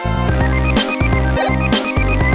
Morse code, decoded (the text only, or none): none